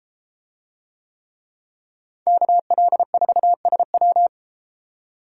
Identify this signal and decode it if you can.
Morse code, heard as a keyed tone, about 32 wpm, 700 Hz. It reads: KL4SW